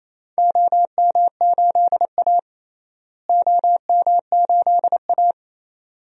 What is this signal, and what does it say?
Morse code, heard as a keyed tone, about 28 wpm, 700 Hz. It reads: OM8A OM8A
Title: DXCC-100 Call Signs - 4 Characters - Repeated 2x - 28wpm